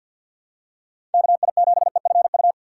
Morse code, heard as a keyed tone, about 52 wpm, 700 Hz. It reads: KI7EFU
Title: USA Call Signs - 6 Characters - 4x Character Spacing - 50wpm